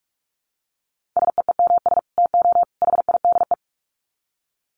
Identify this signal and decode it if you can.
Morse code, heard as a keyed tone, about 44 wpm, 700 Hz. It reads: SEEMS TO HIDE
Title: Dit Challenge 45wpm